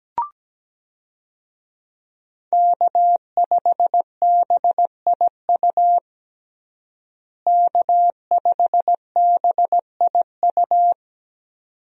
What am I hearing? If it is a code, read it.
Morse code, heard as a keyed tone, about 17 wpm, 700 Hz. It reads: K5BIU K5BIU
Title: USA Call Signs - 5 Characters - Repeated 2x - 17wpm